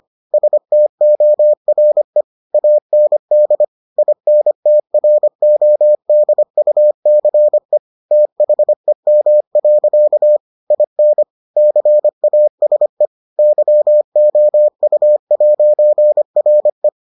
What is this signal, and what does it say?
Morse code, heard as a keyed tone, about 25 wpm, 600 Hz. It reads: STORE AND INTRODUCE THEM. IN CASE YOU'RE